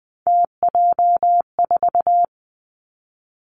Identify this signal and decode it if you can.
Morse code, heard as a keyed tone, about 20 wpm, 700 Hz. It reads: TJ4
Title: Three Challenging Letter-Number Combinations 20wpm